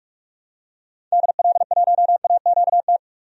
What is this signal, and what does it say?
Morse code, heard as a keyed tone, about 45 wpm, 700 Hz. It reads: DL1AQT
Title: DXCC-100 Call Signs - 6 Characters - 45wpm